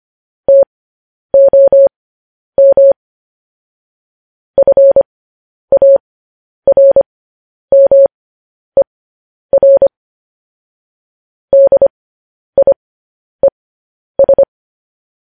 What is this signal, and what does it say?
Morse code, heard as a keyed tone, about 25 wpm, spaced as for 10 wpm, 550 Hz. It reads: TOM FARMER DIES
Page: news-headlines-in-morse-code-at-10-wpm